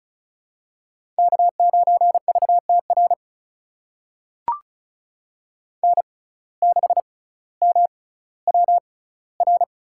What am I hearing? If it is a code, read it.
Morse code, heard as a keyed tone, about 34 wpm, spaced as for 18 wpm, 700 Hz. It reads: K9VTR N6MWR
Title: USA Call Signs - 5 Characters - 6x Character Spacing - 35wpm